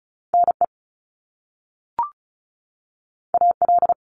Morse code, heard as a keyed tone, about 36 wpm, 700 Hz. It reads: NE AL
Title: US States 35wpm